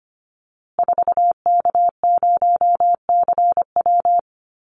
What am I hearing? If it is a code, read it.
Morse code, heard as a keyed tone, about 25 wpm, 700 Hz. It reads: 4K0CW
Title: Call Signs Repeated 3x - 25wpm